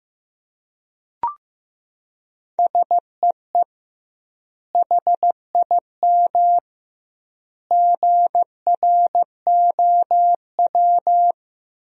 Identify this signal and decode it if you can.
Morse code, heard as a keyed tone, about 15 wpm, 700 Hz. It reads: SEE HIM GROW
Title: 2x Word-Spacing - Sentences from Top 300 Words - 15wpm